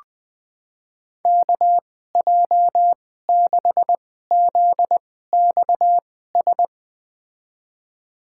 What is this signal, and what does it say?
Morse code, heard as a keyed tone, about 20 wpm, spaced as for 14 wpm, 700 Hz. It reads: KJ6ZXS
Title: USA Call Signs - 6 Characters - 2x Character Spacing - 20wpm